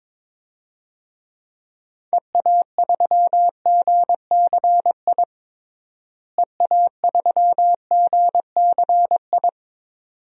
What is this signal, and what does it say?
Morse code, heard as a keyed tone, about 21 wpm, 700 Hz. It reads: EA3GCI EA3GCI
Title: DXCC-100 Call Signs - 6 Characters - Repeated 2x - 22wpm